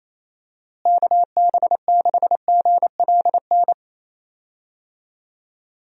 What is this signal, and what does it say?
Morse code, heard as a keyed tone, about 28 wpm, 700 Hz. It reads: KB6GLN
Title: USA Call Signs - 6 Characters - Repeated 3x - 28wpm